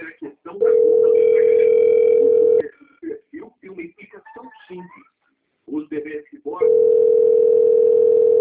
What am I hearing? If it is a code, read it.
Ringback tone (North America), 440 + 480 Hz, with speech.
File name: out- - - - - .87251